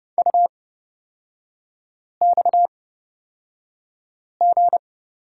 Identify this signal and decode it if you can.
Morse code, heard as a keyed tone, about 31 wpm, spaced as for 5 wpm, 690 Hz. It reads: UXG
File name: Lesson 37e - Letter x - Three Characters 30wpm